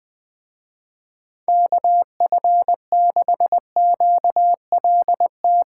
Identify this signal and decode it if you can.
Morse code, heard as a keyed tone, about 20 wpm, 700 Hz. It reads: KF6QLT